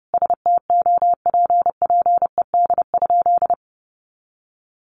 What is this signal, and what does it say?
Morse code, heard as a keyed tone, about 28 wpm, 700 Hz. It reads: STOPPED?
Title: A Sound Of Thunder 30wpm